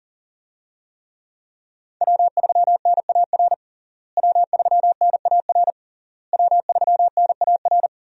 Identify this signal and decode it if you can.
Morse code, heard as a keyed tone, about 40 wpm, 700 Hz. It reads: W3NAR W3NAR W3NAR
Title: USA Call Signs - 5 Characters - Repeated 3x - 40wpm